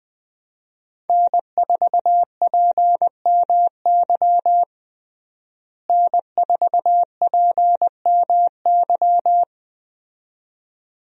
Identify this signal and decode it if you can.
Morse code, heard as a keyed tone, about 20 wpm, 700 Hz. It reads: N4PMY N4PMY
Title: USA Call Signs - 5 Characters - Repeated 2x - 20wpm